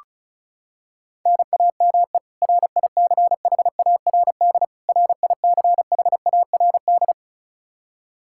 Morse code, heard as a keyed tone, about 36 wpm, 700 Hz. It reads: NAME RICHARD RICHARD
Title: Practice QSO - Exchange 1 - Beginning - 35wpm